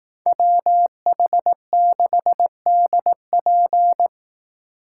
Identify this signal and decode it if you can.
Morse code, heard as a keyed tone, about 18 wpm, 700 Hz. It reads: WH6DP
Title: Speed Racing - USA 2x2 Call Signs 15wpm